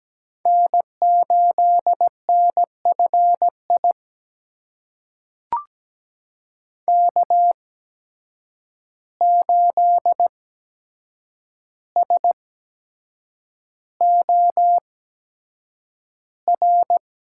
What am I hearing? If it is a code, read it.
Morse code, heard as a keyed tone, about 17 wpm, spaced as for 7 wpm, 700 Hz. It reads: N8NFI K8SOR
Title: USA Call Signs - 5 Characters - 8x Character Spacing - 17wpm